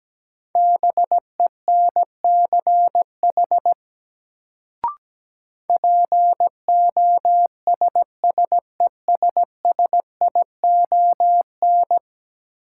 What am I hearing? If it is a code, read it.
Morse code, heard as a keyed tone, about 17 wpm, 700 Hz. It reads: BENCH POSSESSION
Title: Sentences from Top 1400 Words - New Word Drill Alt - 17wpm